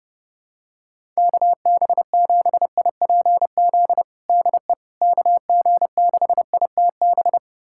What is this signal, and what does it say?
Morse code, heard as a keyed tone, about 30 wpm, 700 Hz. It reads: KB7IPZ DE KG6ITB
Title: USA Callsign DE Callsign 30wpm